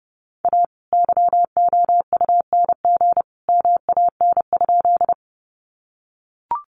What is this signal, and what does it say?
Morse code, heard as a keyed tone, about 30 wpm, 700 Hz. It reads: A YOUNG MAN?
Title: Sentences from Top 400 Words 30wpm